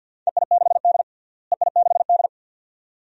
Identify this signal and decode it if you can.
Morse code, heard as a keyed tone, about 53 wpm, 700 Hz. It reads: EI6D EI6D
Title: DXCC-100 Call Signs - 4 Characters - Repeated 2x - 50wpm